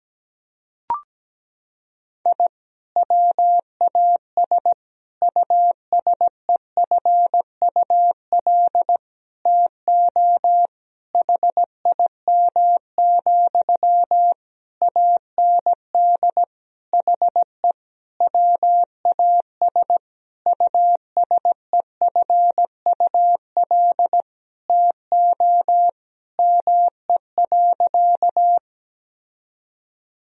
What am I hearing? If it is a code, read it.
Morse code, heard as a keyed tone, about 17 wpm, 700 Hz. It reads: I WAS USEFUL TO HIM, AND HE WAS USEFUL TO ME.
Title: Sentences from Top 1000 Words - Review Part 4 - 17wpm